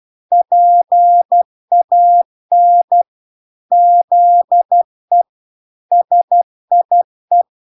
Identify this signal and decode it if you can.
Morse code, heard as a keyed tone, about 12 wpm, 700 Hz. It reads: PAN ZE SIE